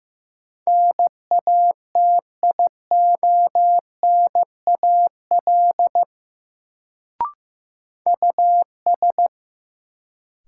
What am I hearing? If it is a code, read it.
Morse code, heard as a keyed tone, about 15 wpm, 700 Hz. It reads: NATIONAL US
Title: Sets of 2 Words - New Word Drill - 15wpm